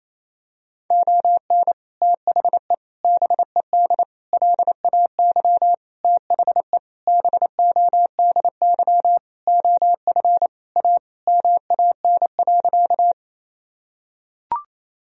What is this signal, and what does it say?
Morse code, heard as a keyed tone, about 28 wpm, 700 Hz. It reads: ON THE BED LAY THE BODY OF A MAN.
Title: Sentences from Top 1000 Words - Review Part 2 - 28wpm